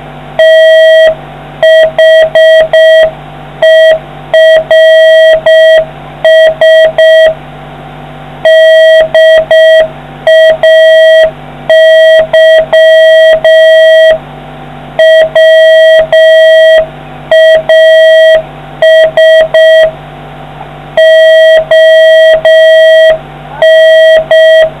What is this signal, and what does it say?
Morse code, heard as a keyed tone, about 6 wpm, 1950 Hz, with noise. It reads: THERS DAY WAS ON